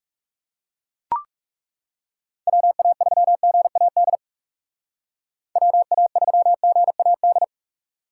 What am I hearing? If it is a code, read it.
Morse code, heard as a keyed tone, about 39 wpm, 700 Hz. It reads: WA3GAD WA3GAD